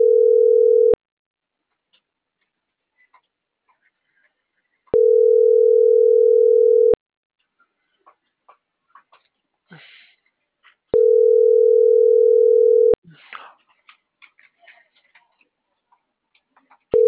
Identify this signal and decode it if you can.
Ringback tone (North America), 440 + 480 Hz.
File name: CENTRAL MAERSK / out luis.herrera - .30572